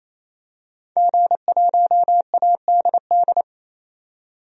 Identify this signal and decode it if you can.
Morse code, heard as a keyed tone, about 28 wpm, 700 Hz. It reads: G1ADD
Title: DXCC-100 Call Signs - 5 Characters - 28wpm